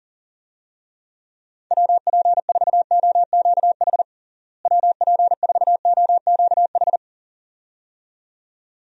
Morse code, heard as a keyed tone, about 37 wpm, 700 Hz. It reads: WP4OQH WP4OQH
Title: USA Call Signs - 6 Characters - Repeated 2x - 40wpm